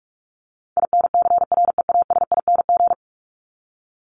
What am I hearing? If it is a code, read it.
Morse code, heard as a keyed tone, about 44 wpm, 700 Hz. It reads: INCREASING
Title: ING Suffix - Top 100 Words with Base Words 45wpm